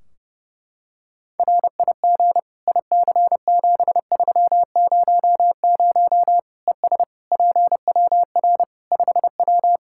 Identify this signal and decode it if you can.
Morse code, heard as a keyed tone, about 30 wpm, 700 Hz. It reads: RIG IC7300 ES PWR 5W
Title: K1USN SST QSO Practice 30wpm